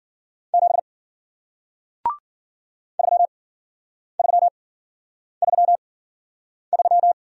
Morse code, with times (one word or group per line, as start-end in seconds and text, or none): 0.5-0.8 7
3.0-7.1 3333